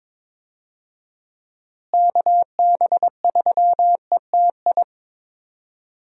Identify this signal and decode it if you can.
Morse code, heard as a keyed tone, about 22 wpm, 700 Hz. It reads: KB3ETI
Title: USA Call Signs - 6 Characters - Repeated 3x - 22wpm